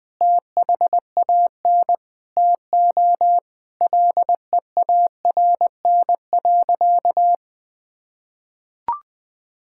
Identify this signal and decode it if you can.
Morse code, heard as a keyed tone, about 20 wpm, 700 Hz. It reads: THAN TO LEARN.
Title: Sentences with 7-Word Phrases - 20wpm